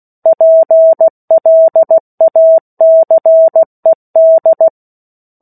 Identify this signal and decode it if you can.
Morse code, heard as a keyed tone, about 16 wpm, 650 Hz. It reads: PLACED
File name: 03-Chapter-03 16 WPM